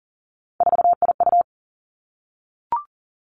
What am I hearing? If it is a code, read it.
Morse code, heard as a keyed tone, about 38 wpm, 700 Hz. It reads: 4IU
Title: Three Challenging Letter-Number Combinations 40wpm